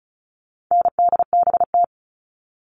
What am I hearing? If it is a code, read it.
Morse code, heard as a keyed tone, about 34 wpm, 700 Hz. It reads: NDBT
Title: All Cut Numbers - 4-Digit 35wpm